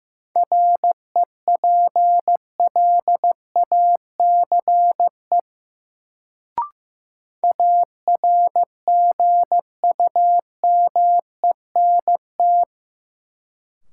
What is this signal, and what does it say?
Morse code, heard as a keyed tone, about 15 wpm, 700 Hz. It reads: REPLACE ARGUMENT
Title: Sentences from Top 1400 Words - New Word Drill Alt - 15wpm